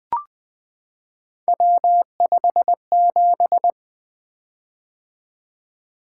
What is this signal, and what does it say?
Morse code, heard as a keyed tone, about 20 wpm, 700 Hz. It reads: W57
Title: Three Letter-Number Combinations - 20wpm